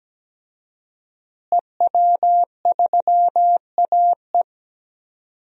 Morse code, heard as a keyed tone, about 17 wpm, 700 Hz. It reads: EW3AE
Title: DXCC-100 Call Signs - 5 Characters - Repeated 2x - 17wpm